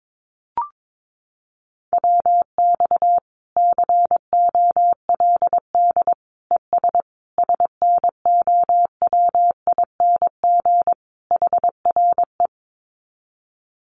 Morse code, heard as a keyed tone, about 22 wpm, 700 Hz. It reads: WX COLD ES SNOWING HRE
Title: Common QSO Phrases 22wpm